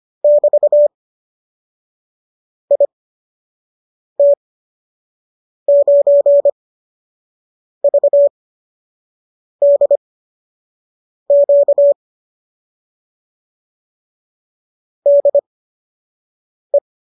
Morse code, heard as a keyed tone, about 25 wpm, spaced as for 6 wpm, 590 Hz. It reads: <BT> IT9VDQ DE